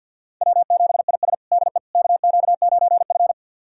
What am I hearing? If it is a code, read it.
Morse code, heard as a keyed tone, about 49 wpm, 700 Hz. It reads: W7IS DE KQ9F
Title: USA Callsign DE Callsign - 4 Characters - 50wpm